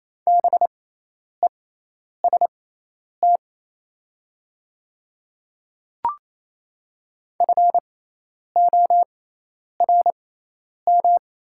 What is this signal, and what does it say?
Morse code, heard as a keyed tone, about 28 wpm, spaced as for 10 wpm, 690 Hz. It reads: BEST FORM
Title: 6x Character Spacing - Top 500 Words - 28wpm